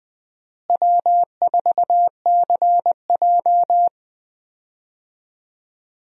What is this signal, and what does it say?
Morse code, heard as a keyed tone, about 20 wpm, 700 Hz. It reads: W4CJ
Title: USA Call Signs - 4 Characters - Repeated 2x - 20wpm